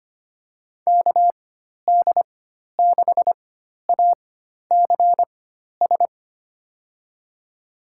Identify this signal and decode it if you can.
Morse code, heard as a keyed tone, about 25 wpm, spaced as for 12 wpm, 700 Hz. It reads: KD6ACS
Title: USA Call Signs - 4x Character Spacing - 25wpm